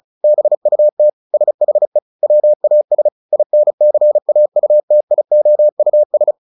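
Morse code, heard as a keyed tone, about 34 wpm, 600 Hz. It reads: BUT SHE WAS INCAUTIOUS
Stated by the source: Morse practice at 35 wpm from CW med Gnister, sending Ken Follett's novel Never in English.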